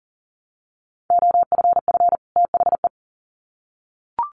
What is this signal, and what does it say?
Morse code, heard as a keyed tone, about 40 wpm, 700 Hz. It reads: OFF THE
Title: Sets of 2 Words 40wpm